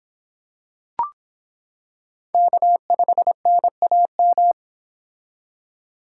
Morse code, heard as a keyed tone, about 26 wpm, 700 Hz. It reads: K5NAM